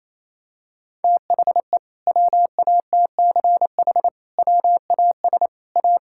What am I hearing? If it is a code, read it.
Morse code, heard as a keyed tone, about 27 wpm, 700 Hz. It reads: THE WATCH WAS A
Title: Sentences from Top 1000 Words - Review Part 6 - 28wpm